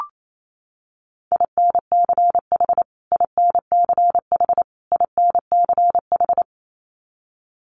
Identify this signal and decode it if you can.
Morse code, heard as a keyed tone, about 28 wpm, 700 Hz. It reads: INCH INCH INCH
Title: Top 500 Words Repeated 28wpm